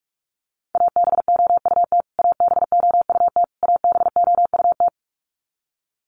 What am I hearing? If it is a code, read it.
Morse code, heard as a keyed tone, about 44 wpm, 700 Hz. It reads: ABOUT ABOUT ABOUT
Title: Top 500 Words Repeated 45wpm